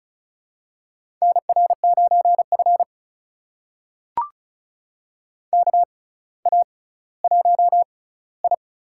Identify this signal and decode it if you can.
Morse code, heard as a keyed tone, about 36 wpm, spaced as for 18 wpm, 700 Hz. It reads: NR9F KA1I
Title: USA Call Signs - 4 Characters - 6x Character Spacing - 35wpm